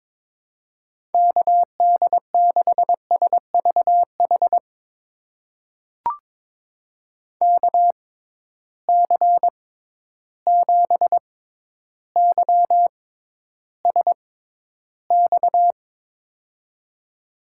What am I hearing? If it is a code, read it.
Morse code, heard as a keyed tone, about 22 wpm, spaced as for 11 wpm, 700 Hz. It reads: KD6SVH KC7YSX
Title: USA Call Signs - 6 Characters - 6x Character Spacing - 22wpm